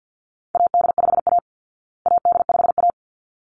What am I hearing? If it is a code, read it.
Morse code, heard as a keyed tone, about 48 wpm, 700 Hz. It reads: AD5A AD5A
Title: Call Signs Repeated 3x - 50wpm